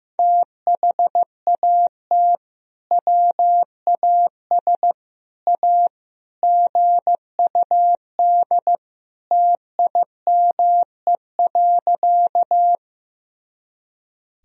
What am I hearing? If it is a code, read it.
Morse code, heard as a keyed tone, about 15 wpm, 700 Hz. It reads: THAT WAS A GUD TIME.
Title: Sentences from Top 200 Words and Abbreviations 15wpm